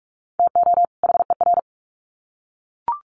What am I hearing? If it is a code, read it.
Morse code, heard as a keyed tone, about 44 wpm, 700 Hz. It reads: TO HER
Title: Sets of 2 Words 45wpm